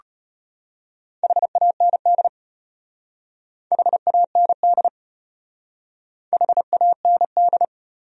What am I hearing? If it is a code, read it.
Morse code, heard as a keyed tone, about 34 wpm, 700 Hz. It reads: HAND HAND HAND